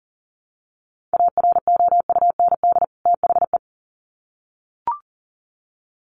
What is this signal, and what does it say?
Morse code, heard as a keyed tone, about 37 wpm, 700 Hz. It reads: AROUND THE
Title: Sets of 2 Words 40wpm